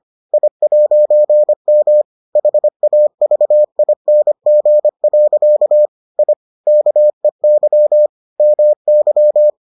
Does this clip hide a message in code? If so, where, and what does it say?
Yes: Morse code from 0.3 to 9.5 s, heard as a keyed tone, about 25 wpm, 600 Hz. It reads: I'M HAVING. I KEY MY